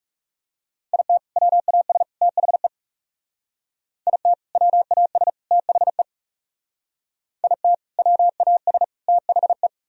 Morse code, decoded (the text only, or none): IT WAS THE IT WAS THE IT WAS THE